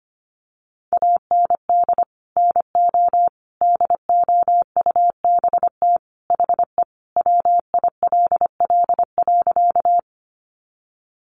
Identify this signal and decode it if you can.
Morse code, heard as a keyed tone, about 25 wpm, 700 Hz. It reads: AND NO DOUBT HE WILL.
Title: Sentences from Top 1500 Words 25wpm